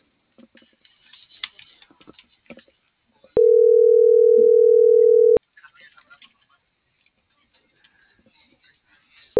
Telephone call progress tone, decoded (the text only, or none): ringback tone